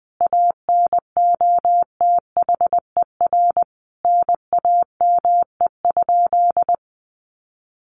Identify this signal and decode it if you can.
Morse code, heard as a keyed tone, about 20 wpm, 700 Hz. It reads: ANOTHER NAME?
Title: Sentences from Top 600 Words 20wpm